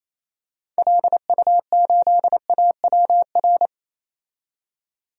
Morse code, heard as a keyed tone, about 28 wpm, 700 Hz. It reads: LU8AWR
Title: DXCC-100 Call Signs - 6 Characters - 28wpm